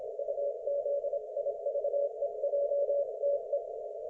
Morse code, heard as a keyed tone, about 26 wpm, 570 Hz. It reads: UZIVEJTE
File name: Archiv / CUC216 / ok1fhi cul216